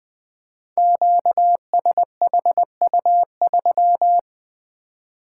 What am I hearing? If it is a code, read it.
Morse code, heard as a keyed tone, about 20 wpm, 700 Hz. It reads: QSHU3
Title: Code Groups - 5 Letter:Numbers - 20wpm